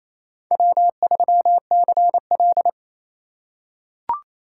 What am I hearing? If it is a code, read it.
Morse code, heard as a keyed tone, about 28 wpm, 700 Hz. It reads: W3CL
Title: USA Call Signs - 4 Characters - 4x Character Spacing - 28wpm